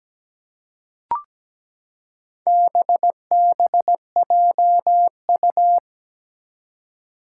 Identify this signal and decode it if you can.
Morse code, heard as a keyed tone, about 17 wpm, 700 Hz. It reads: BBJU
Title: Four Challenging Letter-Number Combinations 17wpm